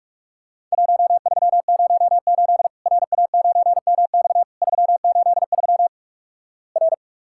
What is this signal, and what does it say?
Morse code, heard as a keyed tone, about 46 wpm, 700 Hz. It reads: 1308 RA9MX 383 R
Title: Worked All Europe DX Contest - QTC Exchange - 45wpm